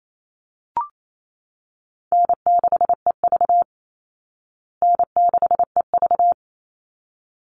Morse code, heard as a keyed tone, about 28 wpm, 700 Hz. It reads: N6EV N6EV
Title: Easy Call Signs Repeated 2x - 28wpm